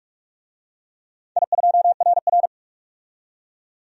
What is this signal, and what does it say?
Morse code, heard as a keyed tone, about 44 wpm, 690 Hz. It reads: I2RR
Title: DXCC-100 Call Signs - 4 Characters - 45wpm